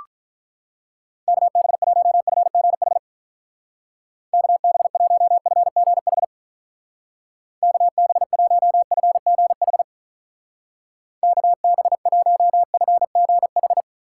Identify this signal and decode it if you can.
Morse code, heard as a keyed tone, about 45 wpm, 700 Hz. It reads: KB1FGH KB1FGH KB1FGH KB1FGH